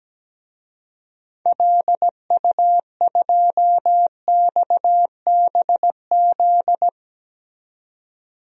Morse code, heard as a keyed tone, about 17 wpm, 700 Hz. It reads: LU2XBZ
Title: DXCC-100 Call Signs - 6 Characters - 17wpm